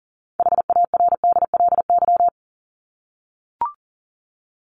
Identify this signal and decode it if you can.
Morse code, heard as a keyed tone, about 40 wpm, 700 Hz. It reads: HARDLY
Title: LY Suffix - Top 100 Words with Base Words 40wpm